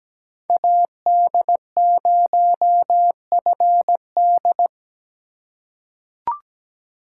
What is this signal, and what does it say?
Morse code, heard as a keyed tone, about 17 wpm, 700 Hz. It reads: AD0FD
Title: USA Call Signs - 4x Character Spacing - 17wpm